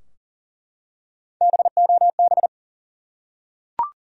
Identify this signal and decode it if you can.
Morse code, heard as a keyed tone, about 40 wpm, 700 Hz. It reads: BOB
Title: USA Common Names - 40wpm